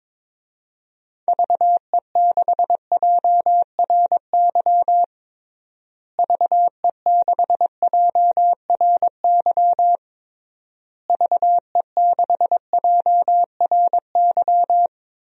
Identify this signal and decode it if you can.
Morse code, heard as a keyed tone, about 22 wpm, 700 Hz. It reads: VE6JRY VE6JRY VE6JRY
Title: DXCC-100 Call Signs - 6 Characters - Repeated 3x - 22wpm